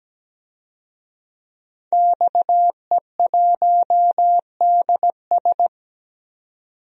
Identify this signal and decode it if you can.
Morse code, heard as a keyed tone, about 17 wpm, 700 Hz. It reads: XE1DS
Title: DXCC-100 Call Signs - 5 Characters - 17wpm